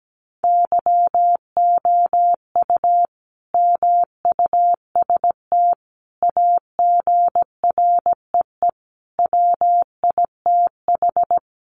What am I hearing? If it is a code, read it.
Morse code, heard as a keyed tone, about 17 wpm, 700 Hz. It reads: YOU MUST AGREE WITH
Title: Sentences from Top 1100 Words 17wpm